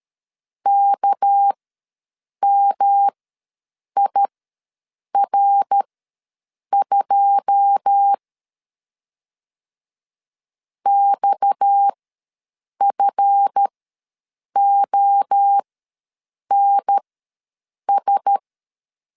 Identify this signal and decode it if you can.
Morse code, heard as a keyed tone, about 13 wpm, spaced as for 7 wpm, 800 Hz. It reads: KMIR2 XFONS
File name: LEZIONI / 9 2 ws 12wpm 3406